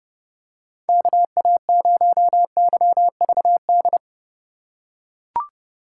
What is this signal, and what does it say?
Morse code, heard as a keyed tone, about 30 wpm, 700 Hz. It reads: KA0YVD
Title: USA Call Signs - 6 Characters - 3x Character Spacing - 30wpm